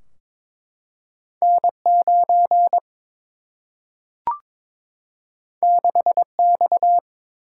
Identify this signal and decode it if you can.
Morse code, heard as a keyed tone, about 21 wpm, 700 Hz. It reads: N 9 6 X
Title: Single-Letter Single-Number 22wpm